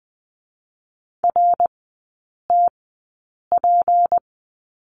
Morse code, heard as a keyed tone, about 20 wpm, spaced as for 8 wpm, 700 Hz. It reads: RTP
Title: 152 Lesson 25e - Letter p - Three Characters 20wpm